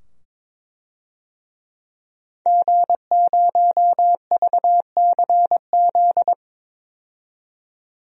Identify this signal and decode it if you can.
Morse code, heard as a keyed tone, about 22 wpm, 700 Hz. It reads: G0VCZ
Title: DXCC-100 Call Signs - 5 Characters - 22wpm